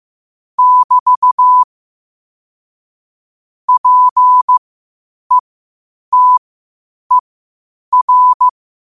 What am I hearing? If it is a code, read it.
Morse code, heard as a keyed tone, about 15 wpm, spaced as for 8 wpm, 1000 Hz. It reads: <BT> PETER
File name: qotdfiles / 8wpm